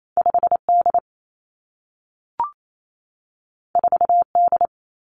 Morse code, heard as a keyed tone, about 28 wpm, 700 Hz. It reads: 5 D 4 D
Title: Two Challenging Letter-Number Combinations 28wpm